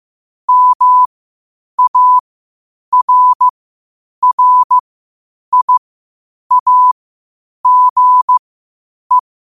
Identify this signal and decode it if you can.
Morse code, heard as a keyed tone, about 15 wpm, spaced as for 8 wpm, 1000 Hz. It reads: MARRIAGE
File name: qotdfiles / 8wpm